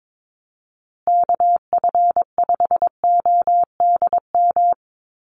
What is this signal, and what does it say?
Morse code, heard as a keyed tone, about 22 wpm, 700 Hz. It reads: KF5ODM
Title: USA Call Signs Repeated 3x 22wpm